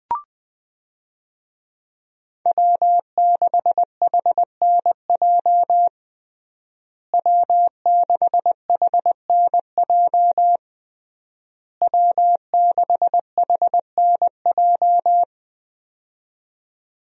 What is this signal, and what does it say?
Morse code, heard as a keyed tone, about 21 wpm, 700 Hz. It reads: W6HNJ W6HNJ W6HNJ
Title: USA Call Signs - 5 Characters - Repeated 3x - 20wpm